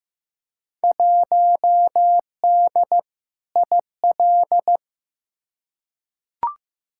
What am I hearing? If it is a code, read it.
Morse code, heard as a keyed tone, about 15 wpm, 700 Hz. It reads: 1D IL
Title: Field Day Exchanges - 15wpm